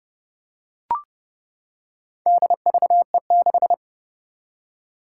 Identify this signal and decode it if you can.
Morse code, heard as a keyed tone, about 31 wpm, 700 Hz. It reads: DVE6
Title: Four Challenging Letter-Number Combinations 30wpm